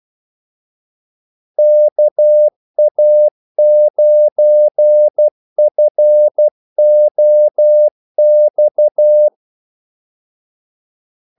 Morse code, heard as a keyed tone, about 12 wpm, 600 Hz. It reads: KA9FOX
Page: 2 ekstra ordmellemrum mellem kaldesignalerne